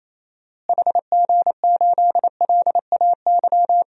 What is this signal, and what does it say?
Morse code, heard as a keyed tone, about 28 wpm, 700 Hz. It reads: HG8LAY
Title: DXCC-100 Call Signs - 6 Characters - Repeated 2x - 28wpm